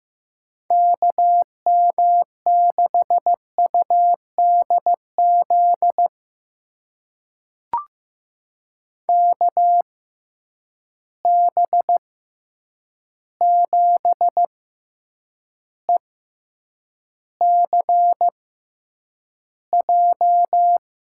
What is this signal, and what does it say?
Morse code, heard as a keyed tone, about 15 wpm, spaced as for 8 wpm, 700 Hz. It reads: KM6UDZ KB7ECJ
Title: USA Call Signs - 6 Characters - 6x Character Spacing - 15wpm